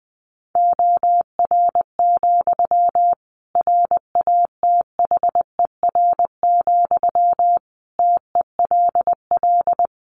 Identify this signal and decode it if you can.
Morse code, heard as a keyed tone, about 20 wpm, 700 Hz. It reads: OR, RATHER, TELL
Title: Sentences from Top 700 Words 20wpm